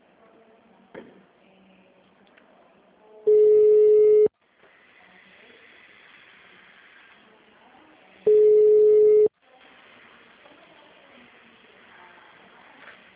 Ringback tone (Europe, 425 Hz).